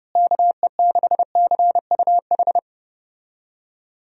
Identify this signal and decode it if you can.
Morse code, heard as a keyed tone, about 32 wpm, 700 Hz. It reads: KE6CUH
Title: USA Call Signs - 6 Characters - Repeated 3x - 30wpm